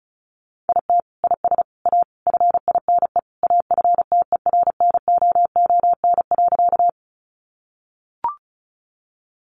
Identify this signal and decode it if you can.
Morse code, heard as a keyed tone, about 34 wpm, 700 Hz. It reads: IT IS A FINE AFTERNOON.
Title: Sentences from Top 1500 Words 35wpm